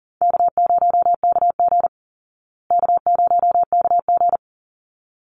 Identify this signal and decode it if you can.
Morse code, heard as a keyed tone, about 40 wpm, 700 Hz. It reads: K0KG K0KG
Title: Easy Call Signs Repeated 2x - 40wpm